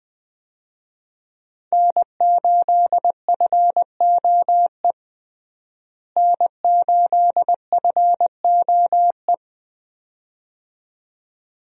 Morse code, heard as a keyed tone, about 20 wpm, 700 Hz. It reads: N8FOE N8FOE
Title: USA Call Signs - 5 Characters - Repeated 2x - 20wpm